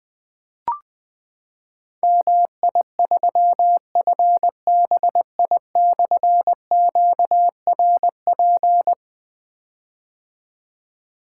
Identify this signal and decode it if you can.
Morse code, heard as a keyed tone, about 20 wpm, 700 Hz. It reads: MI3FBI/QRP
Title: Difficult Call Signs 20wpm